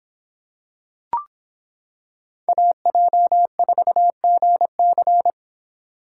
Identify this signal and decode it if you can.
Morse code, heard as a keyed tone, about 26 wpm, 700 Hz. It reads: AJ4GC